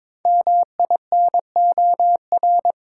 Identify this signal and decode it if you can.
Morse code, heard as a keyed tone, about 23 wpm, 700 Hz. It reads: MINOR
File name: 5 Letter Words - First 3 Letters Given - 22wpm